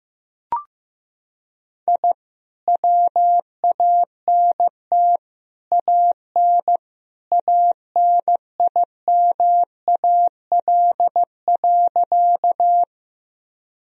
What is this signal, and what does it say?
Morse code, heard as a keyed tone, about 15 wpm, 700 Hz. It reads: I WANT AN ANIMAL.
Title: Sentences from Top 1000 Words - Review Part 1 - 15wpm